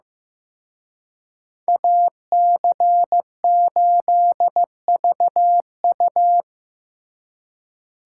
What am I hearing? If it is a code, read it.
Morse code, heard as a keyed tone, about 18 wpm, 700 Hz. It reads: AC8VU
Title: Speed Racing - USA 2x2 Call Signs 15wpm